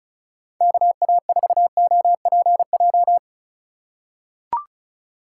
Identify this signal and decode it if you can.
Morse code, heard as a keyed tone, about 36 wpm, 700 Hz. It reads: KA4OPJ
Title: USA Call Signs - 6 Characters - 3x Character Spacing - 35wpm